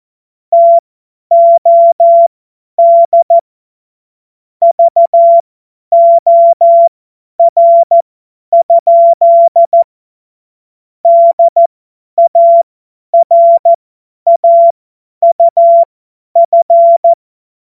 Morse code, heard as a keyed tone, about 14 wpm, spaced as for 10 wpm, 680 Hz. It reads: TOD VOR? DARAUF
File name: Morsekurs / Witze / Witz 26. Oktober 2021, 50 F70